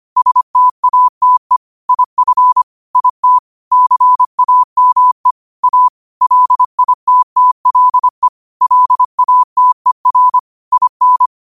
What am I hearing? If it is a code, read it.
Morse code, heard as a keyed tone, about 25 wpm, 1000 Hz. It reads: STATE IF IT CAME A LITTLE LATER IN
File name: qotdfiles / 25wpm